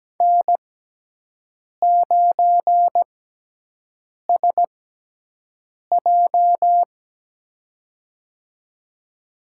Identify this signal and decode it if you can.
Morse code, heard as a keyed tone, about 17 wpm, spaced as for 6 wpm, 700 Hz. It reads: N9SJ
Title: USA Call Signs - 6x Character Spacing - 17wpm